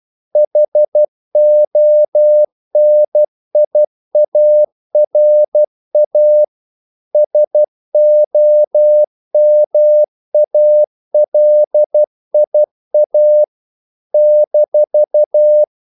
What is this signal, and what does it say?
Morse code, heard as a keyed tone, about 12 wpm, 600 Hz. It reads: HONIARA SOMALIA -
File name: CW / Practice / NationalCapitals12